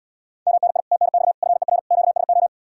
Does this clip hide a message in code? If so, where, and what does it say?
Yes: Morse code from 0.5 to 2.5 s, heard as a keyed tone, about 81 wpm, 700 Hz. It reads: MNI TNX FER QSO